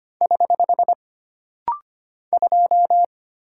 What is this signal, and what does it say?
Morse code, heard as a keyed tone, about 25 wpm, 700 Hz. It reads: <HH> 2
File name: Single Character (Letter:Number:Punctuation:Prosign) - Rapid-Fire - 25wpm